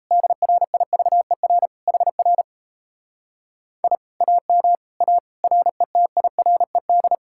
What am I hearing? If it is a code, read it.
Morse code, heard as a keyed tone, about 38 wpm, 700 Hz. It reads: DRIVER HR I AM A RETIRED